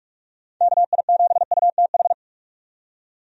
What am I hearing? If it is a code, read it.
Morse code, heard as a keyed tone, about 46 wpm, 700 Hz. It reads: KI7UTH
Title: USA Call Signs - 6 Characters - 45wpm